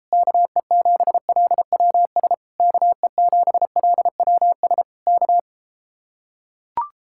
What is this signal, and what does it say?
Morse code, heard as a keyed tone, about 33 wpm, 700 Hz. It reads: KE7LWS KE7LWS K